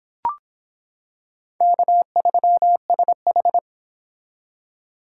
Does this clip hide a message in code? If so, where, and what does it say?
Yes: Morse code from 1.6 to 3.6 s, heard as a keyed tone, about 26 wpm, 700 Hz. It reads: K3SH